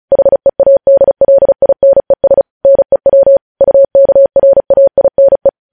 Morse code, heard as a keyed tone, about 36 wpm, 550 Hz. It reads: HEADLINES NEW UKRAINE
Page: News Headlines in Morse Code at 35 WPM